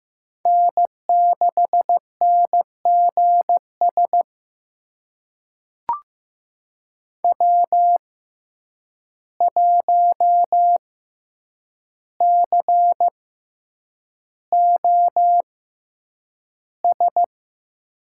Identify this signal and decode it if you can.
Morse code, heard as a keyed tone, about 15 wpm, spaced as for 8 wpm, 700 Hz. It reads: N6NGS W1COS
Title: USA Call Signs - 5 Characters - 6x Character Spacing - 15wpm